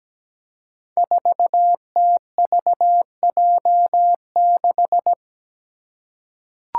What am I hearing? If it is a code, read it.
Morse code, heard as a keyed tone, about 17 wpm, 700 Hz. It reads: 4TVJ6
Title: Five Challenging Letter-Number Combinations 17wpm